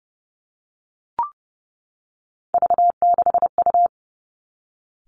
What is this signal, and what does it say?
Morse code, heard as a keyed tone, about 30 wpm, 700 Hz. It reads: V6U
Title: Three Challenging Letter-Number Combinations 30wpm